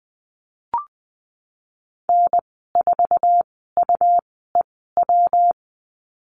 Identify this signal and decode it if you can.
Morse code, heard as a keyed tone, about 20 wpm, spaced as for 15 wpm, 700 Hz. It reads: N4UEW
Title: 140 Lesson 23g - Letter f - Call Signs 20wpm